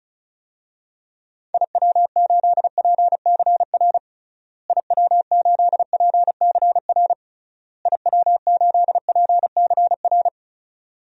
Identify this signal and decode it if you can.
Morse code, heard as a keyed tone, about 36 wpm, 700 Hz. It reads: IW8PCR IW8PCR IW8PCR
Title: DXCC-100 Call Signs - 6 Characters - Repeated 3x - 35wpm